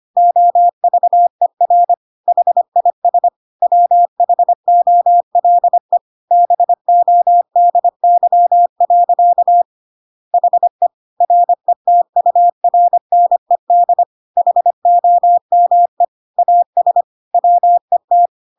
Morse code, heard as a keyed tone, about 25 wpm, 700 Hz. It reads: OVER HIS WHOLE BODY. HE RETURNED HOME AS WET